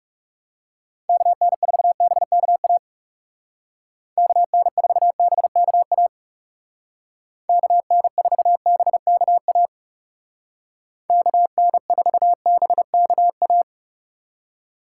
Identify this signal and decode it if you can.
Morse code, heard as a keyed tone, about 40 wpm, 700 Hz. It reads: KN4BKA KN4BKA KN4BKA KN4BKA